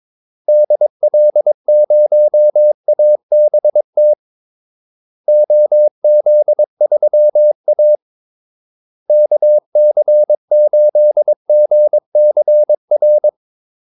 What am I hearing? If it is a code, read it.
Morse code, heard as a keyed tone, about 22 wpm, 600 Hz. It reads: DL0ABT OZ3A KC8GCR
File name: Callsigns-22wpm